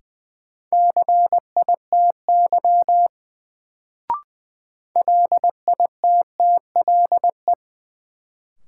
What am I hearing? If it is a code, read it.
Morse code, heard as a keyed tone, about 20 wpm, 700 Hz. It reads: CITY LITTLE
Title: Sets of 2 Words - New Word Drill - 20wpm